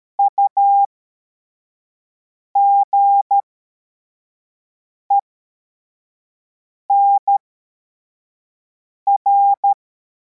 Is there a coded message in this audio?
Yes: Morse code from 0.2 to 9.7 s, heard as a keyed tone, about 13 wpm, spaced as for 4 wpm, 800 Hz. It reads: UGENR